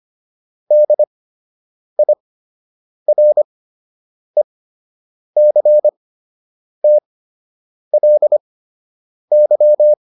Morse code, heard as a keyed tone, about 25 wpm, spaced as for 8 wpm, 600 Hz. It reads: DIRECTLY